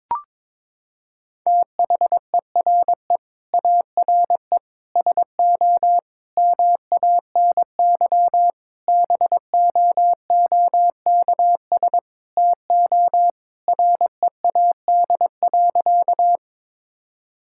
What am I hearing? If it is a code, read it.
Morse code, heard as a keyed tone, about 22 wpm, 700 Hz. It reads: THERE ARE SO MANY BOOKS TO READ.
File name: practice / 22wpm / Sentences with 4-Word Phrases - 22wpm